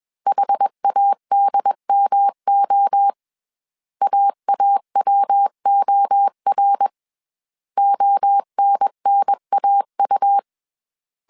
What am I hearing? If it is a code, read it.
Morse code, heard as a keyed tone, about 21 wpm, 800 Hz. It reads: HADMO AAWOR ONNAU